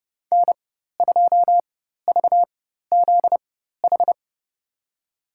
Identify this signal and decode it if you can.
Morse code, heard as a keyed tone, about 32 wpm, spaced as for 14 wpm, 700 Hz. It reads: N2VZH
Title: USA Call Signs - 5 Characters - 4x Character Spacing - 30wpm